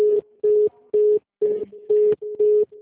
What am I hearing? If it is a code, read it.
Congestion tone (425 Hz).
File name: out- -2010- - - .2